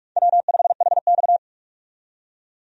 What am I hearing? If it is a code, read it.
Morse code, heard as a keyed tone, about 46 wpm, 700 Hz. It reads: W5HX
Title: USA Call Signs - 4 Characters - 45wpm